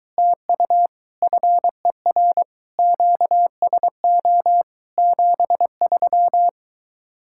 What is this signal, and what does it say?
Morse code, heard as a keyed tone, about 23 wpm, 700 Hz. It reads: TU FER QSO 73